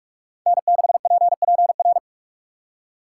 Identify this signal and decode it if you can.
Morse code, heard as a keyed tone, about 44 wpm, 700 Hz. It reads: N6PPR